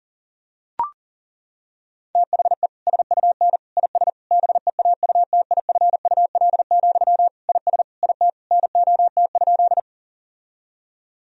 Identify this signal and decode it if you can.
Morse code, heard as a keyed tone, about 37 wpm, 700 Hz. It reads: THE SUN IS BEAUTIFUL, IS IT NOT?
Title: Sentences from Top 1000 Words - Review Part 1 - 40wpm